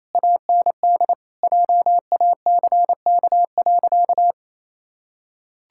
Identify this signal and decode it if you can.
Morse code, heard as a keyed tone, about 28 wpm, 700 Hz. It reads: AND JACK.
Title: Pangram Challenge v2 - 28wpm